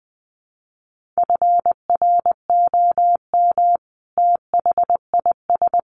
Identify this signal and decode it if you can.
Morse code, heard as a keyed tone, about 20 wpm, 700 Hz. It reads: FROM THIS
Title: Sentences from Top 1500 Words 20wpm